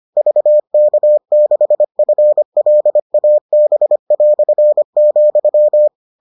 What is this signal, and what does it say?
Morse code, heard as a keyed tone, about 25 wpm, 600 Hz. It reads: VK6FLAB",